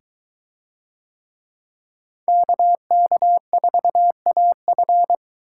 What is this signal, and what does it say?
Morse code, heard as a keyed tone, about 23 wpm, 700 Hz. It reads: KK4AF